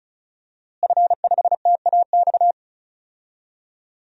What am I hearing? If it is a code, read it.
Morse code, heard as a keyed tone, about 36 wpm, 700 Hz. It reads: F5TAX
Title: DXCC-100 Call Signs - 5 Characters - Repeated 2x - 35wpm